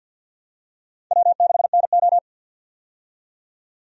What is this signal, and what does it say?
Morse code, heard as a keyed tone, about 52 wpm, 700 Hz. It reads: W6NO
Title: USA Call Signs - 4 Characters - 50wpm